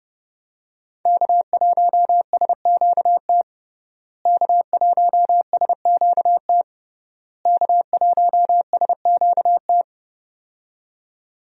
Morse code, heard as a keyed tone, about 30 wpm, 700 Hz. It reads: K1SQT K1SQT K1SQT
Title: USA Call Signs - 5 Characters - Repeated 3x - 30wpm